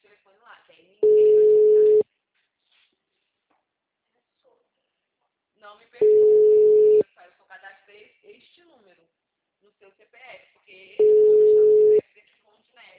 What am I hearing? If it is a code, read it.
Ringback tone (Europe, 425 Hz), with speech.
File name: out- -2018- - - .9969